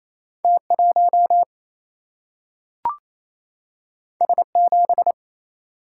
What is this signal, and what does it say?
Morse code, heard as a keyed tone, about 28 wpm, 700 Hz. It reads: T1 S7
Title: Single-Letter Single-Number 28wpm